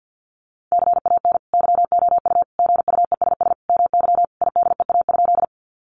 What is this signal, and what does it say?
Morse code, heard as a keyed tone, about 50 wpm, 700 Hz. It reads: CAN YOU GUESS MY IDEA?
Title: Sentences from Top 1500 Words 50wpm